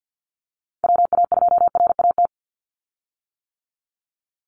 Morse code, heard as a keyed tone, about 50 wpm, 700 Hz. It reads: WA2RAT
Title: USA Call Signs 50wpm